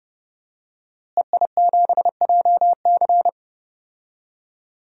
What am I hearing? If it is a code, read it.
Morse code, heard as a keyed tone, about 30 wpm, 700 Hz. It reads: EI7JC
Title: DXCC-100 Call Signs - 5 Characters - Repeated 2x - 30wpm